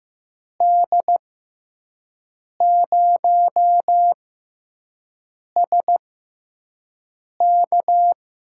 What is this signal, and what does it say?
Morse code, heard as a keyed tone, about 15 wpm, spaced as for 5 wpm, 700 Hz. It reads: D0SK
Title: USA Call Signs - 5 Characters - 6x Character Spacing - 15wpm